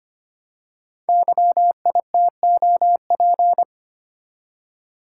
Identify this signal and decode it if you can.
Morse code, heard as a keyed tone, about 25 wpm, 700 Hz. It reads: YITOP
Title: Code Groups - 5 Letters - 25wpm